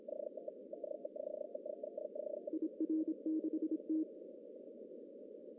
Morse code, heard as a keyed tone, about 29 wpm, 320 Hz. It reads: IR6T
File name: CW / cqww-cw-k5tr